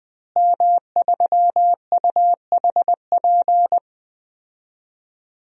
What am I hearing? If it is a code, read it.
Morse code, heard as a keyed tone, about 21 wpm, 700 Hz. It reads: M3UHP
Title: DXCC-100 Call Signs - 5 Characters - Repeated 2x - 20wpm